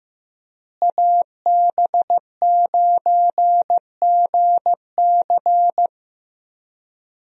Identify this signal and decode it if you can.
Morse code, heard as a keyed tone, about 16 wpm, 700 Hz. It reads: AB9GC
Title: USA Call Signs - 5 Characters - Repeated 2x - 15wpm